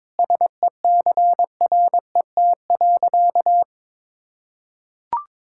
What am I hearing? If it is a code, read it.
Morse code, heard as a keyed tone, about 22 wpm, 700 Hz. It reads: SECRET.
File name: practice / 22wpm / Sentences with 2-Word Phrases - 22wpm